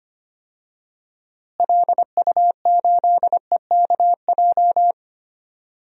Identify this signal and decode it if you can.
Morse code, heard as a keyed tone, about 25 wpm, 700 Hz. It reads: LU8EKJ
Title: DXCC-100 Call Signs - 6 Characters - Repeated 3x - 25wpm